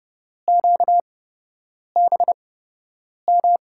Morse code, heard as a keyed tone, about 32 wpm, spaced as for 8 wpm, 700 Hz. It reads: QBM
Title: USA Call Signs - 5 Characters - 8x Character Spacing - 30wpm